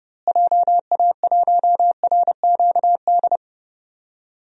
Morse code, heard as a keyed tone, about 30 wpm, 700 Hz. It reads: JA1RQD
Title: DXCC-100 Call Signs - 6 Characters - Repeated 3x - 30wpm